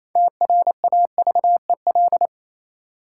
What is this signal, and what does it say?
Morse code, heard as a keyed tone, about 28 wpm, 700 Hz. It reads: TRAVEL
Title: Sentences from Top 400 Words - New Word Drill - 28wpm